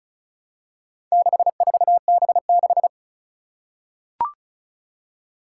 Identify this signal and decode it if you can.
Morse code, heard as a keyed tone, about 36 wpm, 700 Hz. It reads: 64B6
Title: Four Challenging Letter-Number Combinations 35wpm